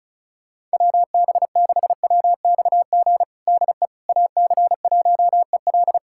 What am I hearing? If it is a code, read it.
Morse code, heard as a keyed tone, about 34 wpm, 700 Hz. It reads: WB6WXG DE AC1EL
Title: USA Callsign DE Callsign 35wpm